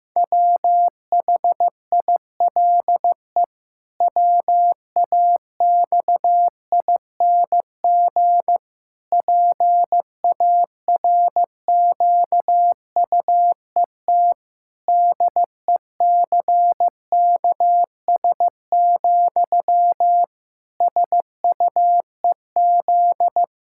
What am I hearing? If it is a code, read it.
Morse code, heard as a keyed tone, about 15 wpm, 700 Hz. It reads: WHILE WAXING PARQUET DECKS, SUEZ
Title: Pangram Challenge v2 - 15wpm